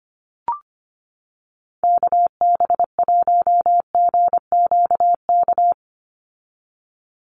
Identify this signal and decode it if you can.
Morse code, heard as a keyed tone, about 25 wpm, 700 Hz. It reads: KB1GQK
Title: USA Call Signs 25wpm